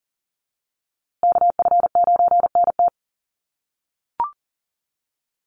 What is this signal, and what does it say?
Morse code, heard as a keyed tone, about 40 wpm, 700 Hz. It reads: KF9NT
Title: USA Call Signs Repeated 2x 40wpm